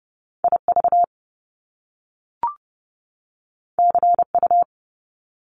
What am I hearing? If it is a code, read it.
Morse code, heard as a keyed tone, about 30 wpm, 700 Hz. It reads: IV CU
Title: Top Two Letter Combinations in Words 30wpm